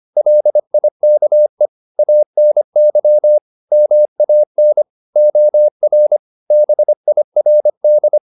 Morse code, heard as a keyed tone, about 25 wpm, 600 Hz. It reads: LIKE ANY MAN OR BIRD
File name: the old man and the sea part 1 25wpm